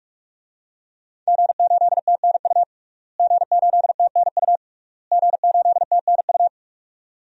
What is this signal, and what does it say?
Morse code, heard as a keyed tone, about 41 wpm, 700 Hz. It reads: G8TNU G8TNU G8TNU
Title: DXCC-100 Call Signs - 5 Characters - Repeated 3x - 45wpm